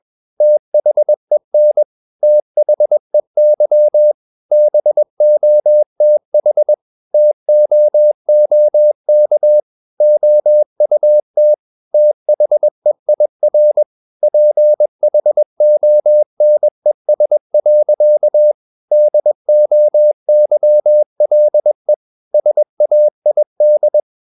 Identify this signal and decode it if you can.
Morse code, heard as a keyed tone, about 21 wpm, 600 Hz. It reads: THEN THEY BOTH TOOK OUT THEIR PHONES. DOYLE SAID
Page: CW med Gnister